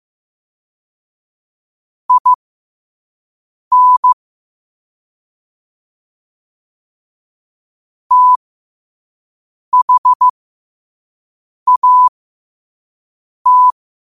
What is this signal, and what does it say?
Morse code, heard as a keyed tone, about 15 wpm, spaced as for 5 wpm, 1000 Hz. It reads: IN THAT
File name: qotdfiles / 5wpm